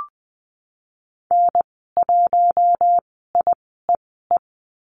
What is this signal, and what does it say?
Morse code, heard as a keyed tone, about 20 wpm, spaced as for 15 wpm, 700 Hz. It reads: N1IEE
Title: 055 Lesson 10g - Letter r - Call Signs 20wpm